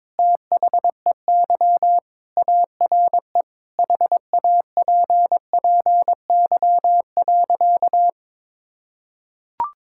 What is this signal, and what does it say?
Morse code, heard as a keyed tone, about 22 wpm, 700 Hz. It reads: THEY ARE HAPPY.
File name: practice / 22wpm / Sentences with 3-Word Phrases - 22wpm